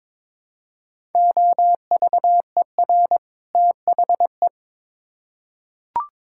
Morse code, heard as a keyed tone, about 22 wpm, 700 Hz. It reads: OVER THE
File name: Build A Phrase - 2-Word Phrases - 22wpm